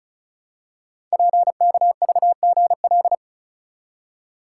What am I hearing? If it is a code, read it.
Morse code, heard as a keyed tone, about 36 wpm, 690 Hz. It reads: PKVGL
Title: Code Groups - 5 Letters - 35wpm